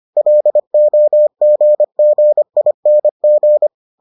Morse code, heard as a keyed tone, about 25 wpm, 600 Hz. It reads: LOGGING